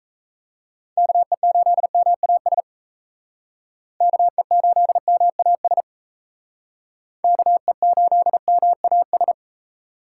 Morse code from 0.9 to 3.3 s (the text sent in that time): KE8MAS